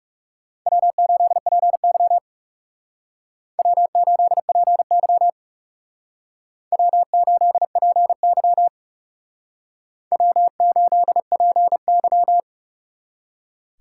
Morse code, heard as a keyed tone, about 36 wpm, 700 Hz. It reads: W8PY W8PY W8PY W8PY
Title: Speed Racing - USA 1x2 Call Signs 30wpm